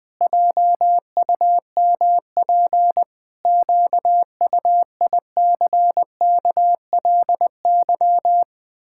Morse code, heard as a keyed tone, about 20 wpm, 700 Hz. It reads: JUMP QUICKLY
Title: Pangram Challenge v2 - 20wpm